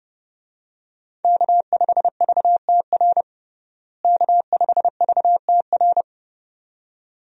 Morse code, heard as a keyed tone, about 30 wpm, 700 Hz. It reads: K5VTR K5VTR
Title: USA Call Signs - 5 Characters - Repeated 2x - 30wpm